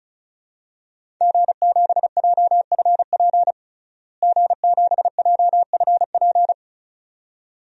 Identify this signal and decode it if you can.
Morse code, heard as a keyed tone, about 35 wpm, 700 Hz. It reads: G7JFP G7JFP
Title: DXCC-100 Call Signs - 5 Characters - Repeated 2x - 35wpm